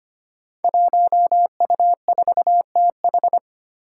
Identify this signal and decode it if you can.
Morse code, heard as a keyed tone, about 25 wpm, 700 Hz. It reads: 1U4TH
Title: Five Challenging Letter-Number Combinations 25wpm